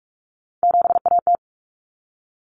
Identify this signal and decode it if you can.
Morse code, heard as a keyed tone, about 44 wpm, 700 Hz. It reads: 7AT
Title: Common Cut Numbers - 3-Digit 45wpm